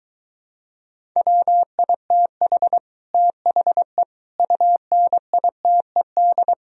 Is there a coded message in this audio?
Yes: Morse code from 1.2 to 6.5 s, heard as a keyed tone, about 23 wpm, 700 Hz. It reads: WITH THE UNITED